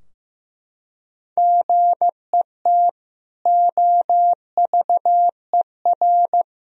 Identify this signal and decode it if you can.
Morse code, heard as a keyed tone, about 15 wpm, 700 Hz. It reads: GET OVER
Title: Sentences from Top 1000 Words - Review Part 2 - 15wpm